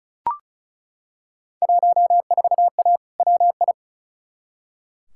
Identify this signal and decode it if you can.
Morse code, heard as a keyed tone, about 35 wpm, 700 Hz. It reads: 14A WI
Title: Field Day Exchanges Encore - 35wpm